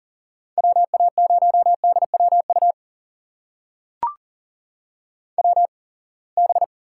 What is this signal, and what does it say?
Morse code, heard as a keyed tone, about 37 wpm, spaced as for 27 wpm, 700 Hz. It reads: WA0DWU WB
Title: USA Call Signs - 6 Characters - 8x Character Spacing - 40wpm